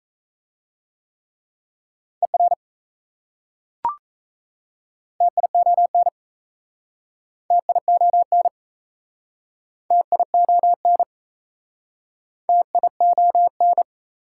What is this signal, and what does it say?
Morse code, heard as a keyed tone, about 35 wpm, 700 Hz. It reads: ER TION TION TION TION